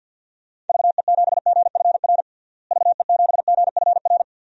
Morse code, heard as a keyed tone, about 52 wpm, 700 Hz. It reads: VE7GFR VE7GFR
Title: DXCC-100 Call Signs - 6 Characters - Repeated 3x - 50wpm